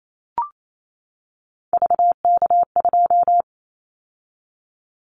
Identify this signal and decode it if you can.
Morse code, heard as a keyed tone, about 28 wpm, 700 Hz. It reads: VK2
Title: Callsign Prefixes for Top 100 DXCC - 28wpm